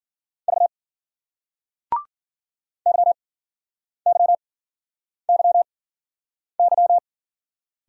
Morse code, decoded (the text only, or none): V YYYY